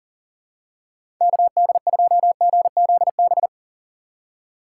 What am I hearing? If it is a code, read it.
Morse code, heard as a keyed tone, about 40 wpm, 700 Hz. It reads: KD2GZB
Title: USA Call Signs - 6 Characters - 4x Character Spacing - 40wpm